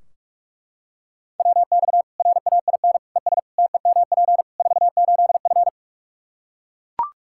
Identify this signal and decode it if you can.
Morse code, heard as a keyed tone, about 42 wpm, 700 Hz. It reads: WX RAIN ES TEMP 48F
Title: Practice QSO - Exchange 2 - Info - 45wpm